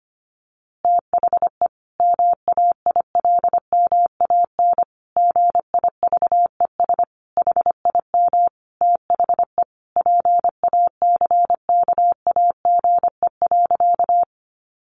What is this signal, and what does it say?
Morse code, heard as a keyed tone, about 25 wpm, 700 Hz. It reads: THE MAILMAN GIVES HIM THE PACKAGE.
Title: Easy Phrases 25wpm